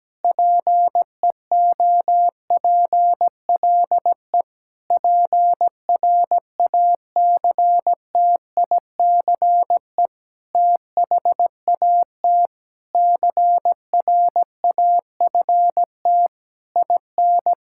Morse code, heard as a keyed tone, about 17 wpm, 700 Hz. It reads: PEOPLE PRACTICE THAT CRAFT IN
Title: Sentences from Top 1000 Words - Review Part 6 - 17wpm